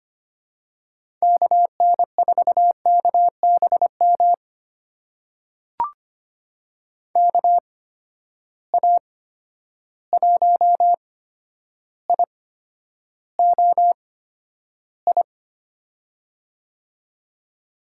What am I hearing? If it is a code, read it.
Morse code, heard as a keyed tone, about 25 wpm, spaced as for 11 wpm, 700 Hz. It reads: KN4KBM KA1IOI
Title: USA Call Signs - 6 Characters - 8x Character Spacing - 25wpm